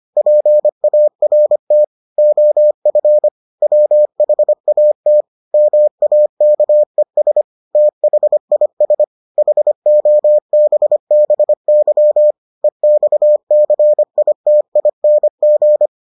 Morse code, heard as a keyed tone, about 25 wpm, 600 Hz. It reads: PART OF WHAT MAKES THIS HOBBY EXCITING